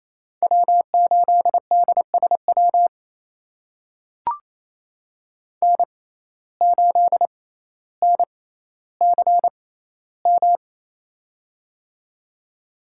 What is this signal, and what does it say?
Morse code, heard as a keyed tone, about 28 wpm, spaced as for 14 wpm, 700 Hz. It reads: W8DSW N8NCM
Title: USA Call Signs - 5 Characters - 6x Character Spacing - 28wpm